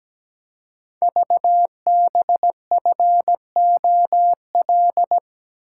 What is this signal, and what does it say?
Morse code, heard as a keyed tone, about 17 wpm, 700 Hz. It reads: VBFOL
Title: Code Groups - 5 Letters - 17wpm